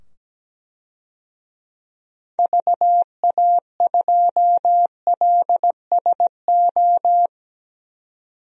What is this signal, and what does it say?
Morse code, heard as a keyed tone, about 17 wpm, 700 Hz. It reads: VA2LSO
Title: DXCC-100 Call Signs - 6 Characters - 17wpm